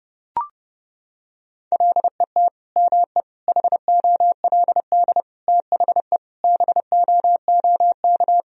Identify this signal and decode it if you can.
Morse code, heard as a keyed tone, about 30 wpm, 700 Hz. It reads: LET ME HOLD THE BOOK
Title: Sentences from Top 1000 Words - Review Part 2 - 30wpm